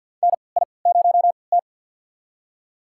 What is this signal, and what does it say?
Morse code, heard as a keyed tone, about 49 wpm, spaced as for 28 wpm, 700 Hz. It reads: NI0T
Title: USA Call Signs - 4 Characters - 3x Character Spacing - 50wpm